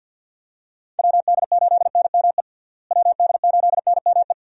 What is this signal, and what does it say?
Morse code, heard as a keyed tone, about 48 wpm, 700 Hz. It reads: WD8NME WD8NME
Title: USA Call Signs - 6 Characters - Repeated 2x - 50wpm